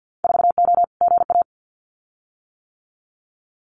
Morse code, heard as a keyed tone, about 50 wpm, 700 Hz. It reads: 4O GA
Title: Winter Field Day Exchanges 50wpm